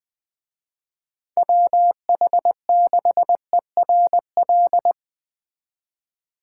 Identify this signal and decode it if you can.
Morse code, heard as a keyed tone, about 22 wpm, 700 Hz. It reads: WH6ERL
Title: Speed Racing - USA 2x3 Call Signs 20wpm